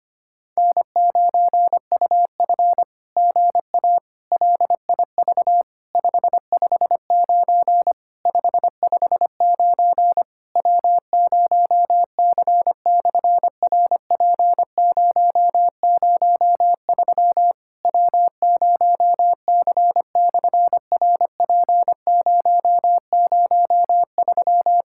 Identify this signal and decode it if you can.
Morse code, heard as a keyed tone, about 25 wpm, 690 Hz. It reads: N9UF GA LIV 559 559 W0C/RP003 W0C/RP003
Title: SOTA QSO Practice 25wpm